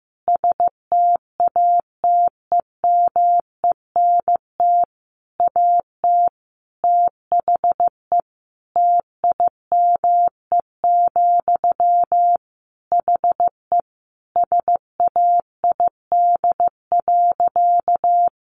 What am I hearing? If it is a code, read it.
Morse code, heard as a keyed tone, about 14 wpm, 700 Hz. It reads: STATEMENT AT THE TIME, HE SAID.
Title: Sentences from Top 1200 Words 15wpm